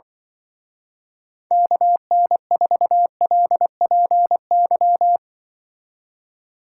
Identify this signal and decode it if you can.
Morse code, heard as a keyed tone, about 24 wpm, 700 Hz. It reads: KN4LPY
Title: Speed Racing - USA Call Signs 20wpm